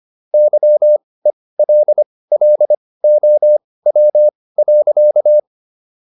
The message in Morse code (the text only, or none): YELLOW.